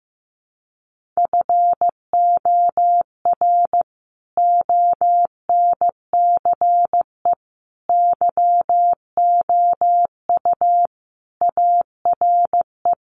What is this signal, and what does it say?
Morse code, heard as a keyed tone, about 15 wpm, 700 Hz. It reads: FOR ONCE YOU ARE
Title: Sentences from Top 300 Words 15wpm